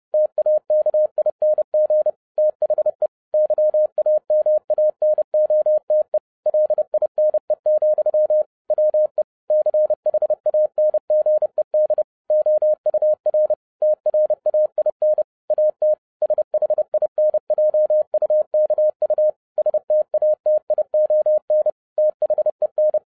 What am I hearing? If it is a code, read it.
Morse code, heard as a keyed tone, about 30 wpm, 610 Hz. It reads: TAKING THE YAMANOTE LINE, WE CHANGED OUR TRAIN AT SHINJUKU STATION THEN